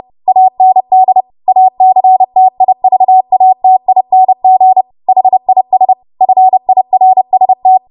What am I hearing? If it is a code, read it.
Morse code, heard as a keyed tone, about 30 wpm, 750 Hz. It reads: AND ACTIVATING HIS FIRST
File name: Morse / 30WPM